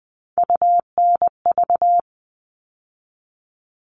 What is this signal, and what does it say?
Morse code, heard as a keyed tone, about 20 wpm, 700 Hz. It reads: UNV
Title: All Cut Numbers - 3-Digit 20wpm